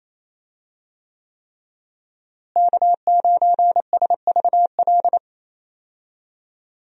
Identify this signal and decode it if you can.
Morse code, heard as a keyed tone, about 28 wpm, 700 Hz. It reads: K9SVL
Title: USA Call Signs - 5 Characters - Repeated 3x - 28wpm